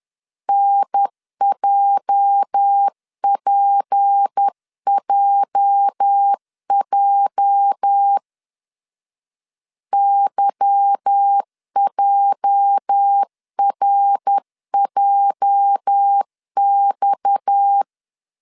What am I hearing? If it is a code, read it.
Morse code, heard as a keyed tone, about 10 wpm, 800 Hz. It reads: NJPJJ YJRJX
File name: LEZIONI / 8 YJ45 / 8 YJ 10wpm 2651